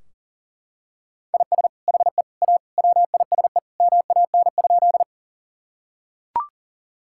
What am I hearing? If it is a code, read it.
Morse code, heard as a keyed tone, about 37 wpm, 700 Hz. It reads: IS HE A WISE MAN?
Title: Sentences from Top 1600 Words 40wpm